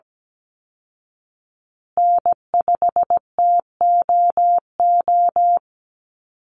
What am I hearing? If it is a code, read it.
Morse code, heard as a keyed tone, about 17 wpm, 700 Hz. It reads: N5TOO
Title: Call Signs Repeated 2x - 17wpm